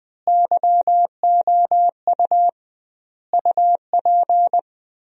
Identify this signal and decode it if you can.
Morse code, heard as a keyed tone, about 21 wpm, 700 Hz. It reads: YOU UP
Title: 2x Word-Spacing - Sentences from Top 100 Words - 20wpm